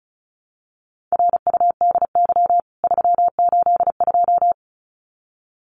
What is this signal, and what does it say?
Morse code, heard as a keyed tone, about 34 wpm, 700 Hz. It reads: RUDY 382
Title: CWT Exchange 35wpm